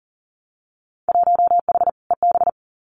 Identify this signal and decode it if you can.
Morse code, heard as a keyed tone, about 42 wpm, 700 Hz. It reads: 1H EB
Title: Winter Field Day Exchanges 40wpm